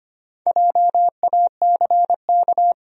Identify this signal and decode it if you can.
Morse code, heard as a keyed tone, about 25 wpm, 700 Hz. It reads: JACK
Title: USA Common Names - Max 5 Letters - 25wpm